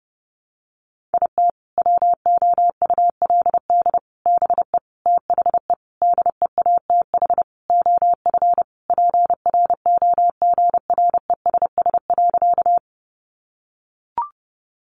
Morse code, heard as a keyed tone, about 30 wpm, 700 Hz. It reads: IT WOULD BE THE DEATH OF PROGRESS.
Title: Sentences from Top 1400 Words 30wpm